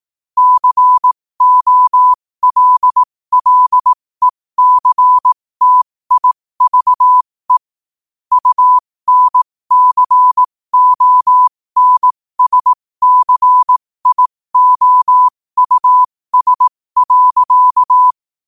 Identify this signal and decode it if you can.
Morse code, heard as a keyed tone, about 18 wpm, spaced as for 15 wpm, 1000 Hz. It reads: COLLECTIVE UNCONSCIOUS.